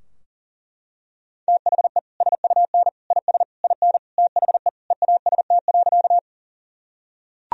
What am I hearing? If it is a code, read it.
Morse code, heard as a keyed tone, about 40 wpm, 700 Hz. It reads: THE SUN IS IN THE EAST.
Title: Sentences from Top 1000 Words - Review Part 3 - 40wpm